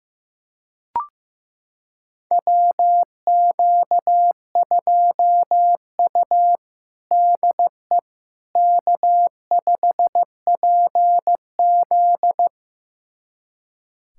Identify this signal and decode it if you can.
Morse code, heard as a keyed tone, about 15 wpm, 700 Hz. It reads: WQ2U DE K5PZ
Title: USA Callsign DE Callsign - 4 Characters - 15wpm